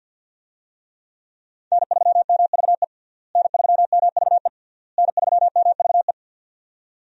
Morse code, heard as a keyed tone, about 53 wpm, 700 Hz. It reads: N3MVE N3MVE N3MVE
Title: USA Call Signs - 5 Characters - Repeated 3x - 50wpm